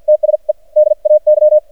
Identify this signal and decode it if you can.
Morse code, heard as a keyed tone, about 53 wpm, spaced as for 43 wpm, 610 Hz. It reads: TSE DAY
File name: archived / CRSnet